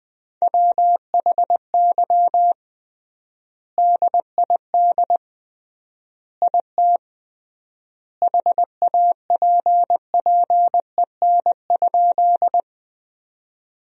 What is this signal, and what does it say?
Morse code, heard as a keyed tone, about 20 wpm, 700 Hz. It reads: WHY DID IT HAPPEN?
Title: 3x Word-Spacing - Sentences from Top 400 Words - 20wpm